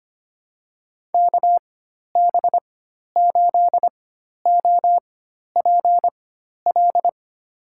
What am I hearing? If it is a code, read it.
Morse code, heard as a keyed tone, about 25 wpm, spaced as for 12 wpm, 700 Hz. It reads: KB8OPL
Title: USA Call Signs - 6 Characters - 4x Character Spacing - 25wpm